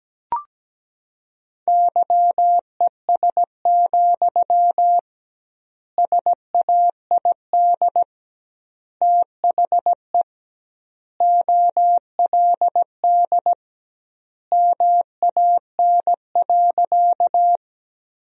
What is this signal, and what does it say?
Morse code, heard as a keyed tone, about 17 wpm, 700 Hz. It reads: YES, SAID THE OLD MAN.
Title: 2x Word-Spacing - Sentences from Top 500 Words - 17wpm